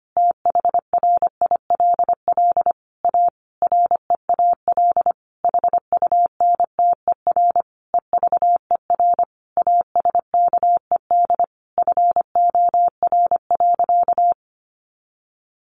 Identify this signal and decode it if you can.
Morse code, heard as a keyed tone, about 25 wpm, 700 Hz. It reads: THRILL A REAL HUNTER EVER ASKED FOR.
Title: A Sound Of Thunder 25wpm